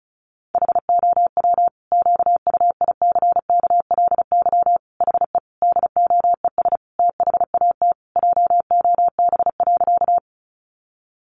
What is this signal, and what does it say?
Morse code, heard as a keyed tone, about 34 wpm, 700 Hz. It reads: HOW QUICKLY HE DOES THAT JOB.
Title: Sentences from Top 700 Words 35wpm